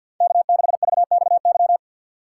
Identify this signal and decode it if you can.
Morse code, heard as a keyed tone, about 52 wpm, 690 Hz. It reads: K6VXY
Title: USA Call Signs - 5 Characters - 6x Character Spacing - 50wpm